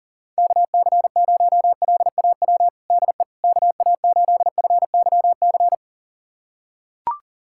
Morse code, heard as a keyed tone, about 40 wpm, 700 Hz. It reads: KC0LAW DE KA8FYC
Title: USA Callsign DE Callsign - 6 Characters - 40wpm